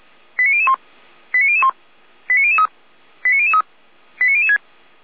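Five-tone selective call (ZVEI), heard as quick runs of five tones, tones of about 80 ms. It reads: 89001 89001 89003 89003 89006
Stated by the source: ZVEI five-tone selective call 89160 received on an SDR, fire brigade (Feuerwehr) Orlamünde.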